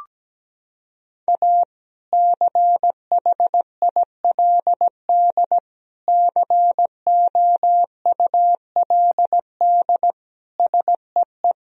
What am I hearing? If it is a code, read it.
Morse code, heard as a keyed tone, about 17 wpm, 700 Hz. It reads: A CHILD COULD SEE
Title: Sentences from Top 1000 Words - Review Part 1 - 17wpm